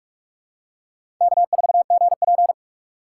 Morse code, heard as a keyed tone, about 46 wpm, 700 Hz. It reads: K4GP
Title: USA Call Signs - 4 Characters - 6x Character Spacing - 45wpm